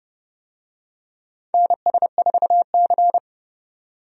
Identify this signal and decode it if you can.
Morse code, heard as a keyed tone, about 29 wpm, 700 Hz. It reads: NS4C
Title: USA Call Signs - 4 Characters - 30wpm